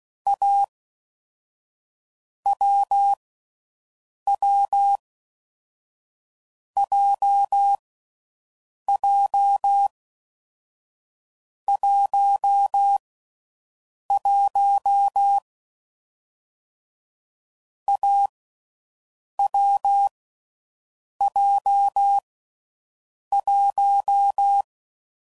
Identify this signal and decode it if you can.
Morse code, heard as a keyed tone, about 16 wpm, 800 Hz. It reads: A W W J J 1 1 A W J 1
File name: Combinations / Difficult Combos / AWJ1